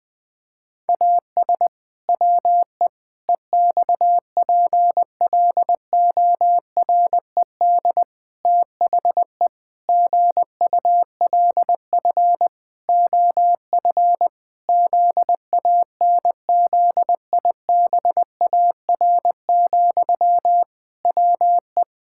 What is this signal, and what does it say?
Morse code, heard as a keyed tone, about 20 wpm, 700 Hz. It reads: AS WE EXPLORED THE GULF OF ZANZIBAR, WE
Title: Pangram Challenge v2 - 20wpm